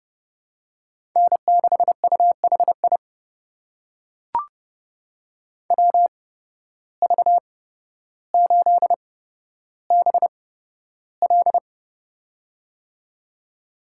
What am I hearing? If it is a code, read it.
Morse code, heard as a keyed tone, about 32 wpm, spaced as for 13 wpm, 700 Hz. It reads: N6UHI WV8BL
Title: USA Call Signs - 5 Characters - 8x Character Spacing - 30wpm